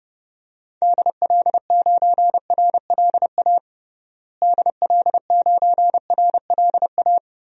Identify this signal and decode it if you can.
Morse code, heard as a keyed tone, about 32 wpm, 700 Hz. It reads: DL9RLA DL9RLA
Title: DXCC-100 Call Signs - 6 Characters - Repeated 3x - 30wpm